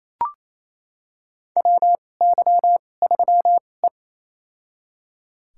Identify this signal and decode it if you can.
Morse code, heard as a keyed tone, about 28 wpm, spaced as for 20 wpm, 700 Hz. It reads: WY3E
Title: USA Call Signs - 4 Characters - 2x Character Spacing - 28wpm